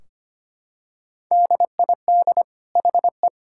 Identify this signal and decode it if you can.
Morse code, heard as a keyed tone, about 25 wpm, 700 Hz. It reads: DID HE
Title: Sentences from Top 1600 Words 25wpm